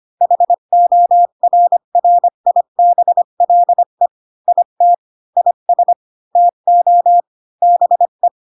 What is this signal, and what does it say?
Morse code, heard as a keyed tone, about 25 wpm, 700 Hz. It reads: HORRIBLE IT IS TO BE
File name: Pinocchio / Pinocchio ch5 20 25 ik1hge